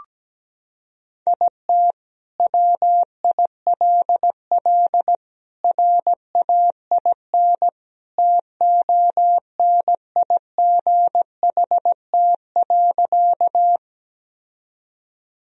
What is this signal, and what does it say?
Morse code, heard as a keyed tone, about 17 wpm, 700 Hz. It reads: IT WILL RAIN TONIGHT.
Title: Sentences from Top 1000 Words - Review Part 1 - 17wpm